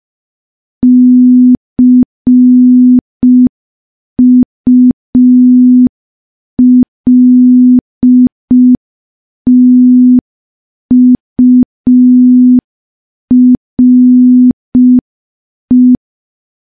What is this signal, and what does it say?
Morse code, heard as a keyed tone, about 5 wpm, 250 Hz. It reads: CULTURE
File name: morse / morse-training-05WPM